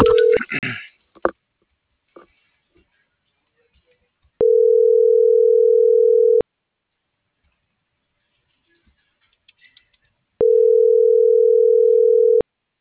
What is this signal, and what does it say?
Ringback tone (North America), 440 + 480 Hz.